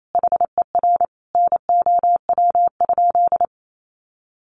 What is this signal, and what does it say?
Morse code, heard as a keyed tone, about 28 wpm, 690 Hz. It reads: HER NOW?
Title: Sentences from Top 900 Words 28wpm